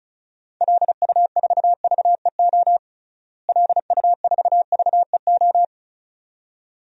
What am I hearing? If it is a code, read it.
Morse code, heard as a keyed tone, about 34 wpm, 700 Hz. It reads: LU4VEO LU4VEO
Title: DXCC-100 Call Signs - 6 Characters - Repeated 2x - 35wpm